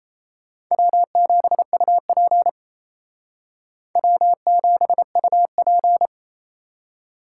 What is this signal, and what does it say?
Morse code, heard as a keyed tone, about 34 wpm, 700 Hz. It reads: W7UP W7UP
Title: Speed Racing - USA 1x2 Call Signs 28wpm